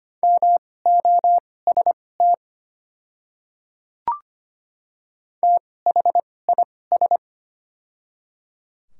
Morse code, heard as a keyed tone, about 25 wpm, spaced as for 18 wpm, 700 Hz. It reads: MOST THIS
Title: 2x Character Spacing - Top 100 Words - 25wpm